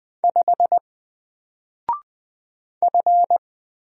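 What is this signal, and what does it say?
Morse code, heard as a keyed tone, about 20 wpm, 700 Hz. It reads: 5 F
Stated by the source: Single letter and number Morse drill at 20 wpm, from N4LCD.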